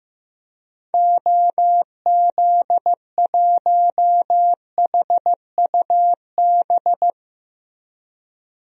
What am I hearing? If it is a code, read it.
Morse code, heard as a keyed tone, about 16 wpm, 700 Hz. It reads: OZ1HUB
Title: DXCC-100 Call Signs - 6 Characters - Repeated 2x - 15wpm